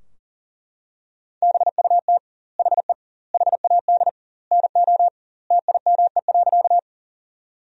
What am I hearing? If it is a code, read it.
Morse code, heard as a keyed tone, about 40 wpm, 700 Hz. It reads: BUT HE HAD NO TIME.
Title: 2x Word-Spacing - Sentences from Top 100 Words - 40wpm